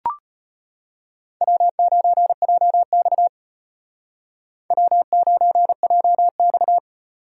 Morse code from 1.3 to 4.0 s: W9JX

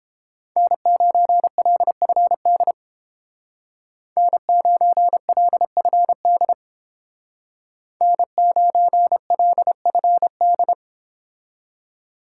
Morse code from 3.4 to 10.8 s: N9LFD N9LFD

